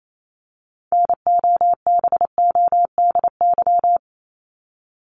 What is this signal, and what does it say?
Morse code, heard as a keyed tone, about 28 wpm, 700 Hz. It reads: NOBODY
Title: Top 1501-1750 Words 28wpm